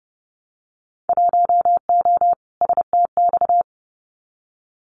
Morse code, heard as a keyed tone, about 29 wpm, 700 Hz. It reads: 1O STX
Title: Winter Field Day Exchanges 30wpm